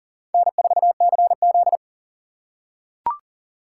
Morse code, heard as a keyed tone, about 40 wpm, 700 Hz. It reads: N4CZ
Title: USA Call Signs - 3x Character Spacing - 40wpm